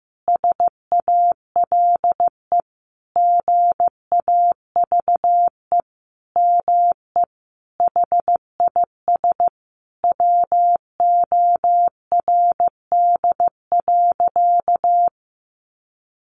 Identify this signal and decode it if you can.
Morse code, heard as a keyed tone, about 15 wpm, 700 Hz. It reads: SALE GAVE ME HIS WORD.
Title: Sentences from Top 1400 Words 15wpm